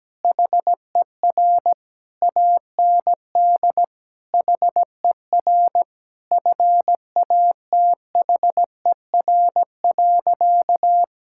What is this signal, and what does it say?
Morse code, heard as a keyed tone, about 17 wpm, 700 Hz. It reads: HER AND HER FATHER.
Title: Sentences from Top 1000 Words - Review Part 6 - 17wpm